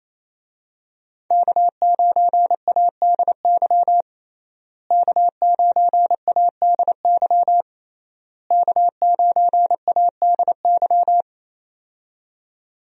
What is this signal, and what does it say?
Morse code, heard as a keyed tone, about 28 wpm, 700 Hz. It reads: K9ADY K9ADY K9ADY
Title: USA Call Signs - 5 Characters - Repeated 3x - 28wpm